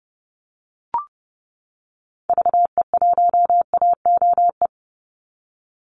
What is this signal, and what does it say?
Morse code, heard as a keyed tone, about 30 wpm, 700 Hz. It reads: VE1AOE
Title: Call Signs - 30wpm